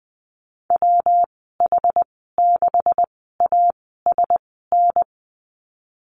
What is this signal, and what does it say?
Morse code, heard as a keyed tone, about 20 wpm, spaced as for 14 wpm, 700 Hz. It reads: WH6ASN
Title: 167 Lesson 27g - Number 7 - Call Signs 20wpm